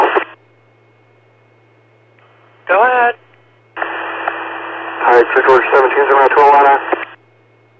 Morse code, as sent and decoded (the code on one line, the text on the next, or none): none